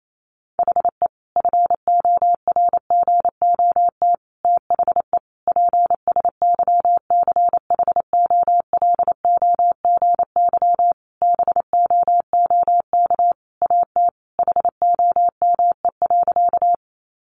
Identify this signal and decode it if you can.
Morse code, heard as a keyed tone, about 28 wpm, 700 Hz. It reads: HE FORGOT THE PSYCHOLOGY BOOK AT HOME.
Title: Sentences from Top 1400 Words 28wpm